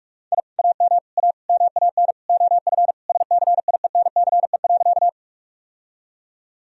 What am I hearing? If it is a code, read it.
Morse code, heard as a keyed tone, about 46 wpm, 700 Hz. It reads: I AM A MAN OF SCIENCE.
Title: Sentences from Top 1000 Words - Review Part 2 - 45wpm